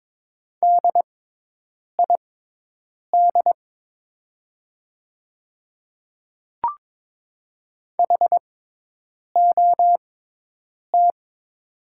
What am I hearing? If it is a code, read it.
Morse code, heard as a keyed tone, about 22 wpm, spaced as for 8 wpm, 700 Hz. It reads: DID HOT
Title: 6x Character Spacing - Top 100 Words - 22wpm